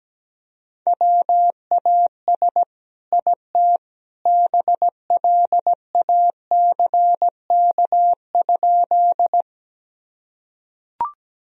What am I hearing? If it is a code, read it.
Morse code, heard as a keyed tone, about 17 wpm, 700 Hz. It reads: WAS IT BLACK?
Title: Sentences from Top 1000 Words - Review Part 2 - 17wpm